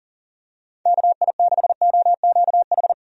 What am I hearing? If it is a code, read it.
Morse code, heard as a keyed tone, about 40 wpm, 700 Hz. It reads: KI6OQH
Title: USA Call Signs - 6 Characters - 40wpm